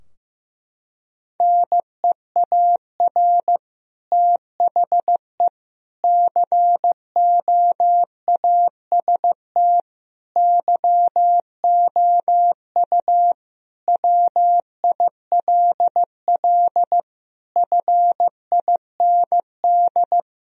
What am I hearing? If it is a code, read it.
Morse code, heard as a keyed tone, about 15 wpm, 700 Hz. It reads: NEAR THE COAST YOU WILL FIND
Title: Sentences from Top 1000 Words - Review Part 4 - 15wpm